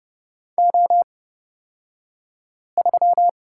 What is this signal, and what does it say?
Morse code, heard as a keyed tone, about 30 wpm, spaced as for 5 wpm, 700 Hz. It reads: O3
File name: Lesson 37d - Letter x - Two Characters 30wpm